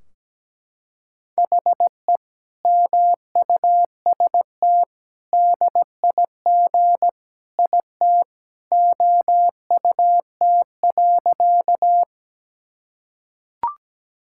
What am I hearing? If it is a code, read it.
Morse code, heard as a keyed tone, about 17 wpm, 700 Hz. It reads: HE MUST DIG IT OUT.
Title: Sentences from Top 1600 Words 17wpm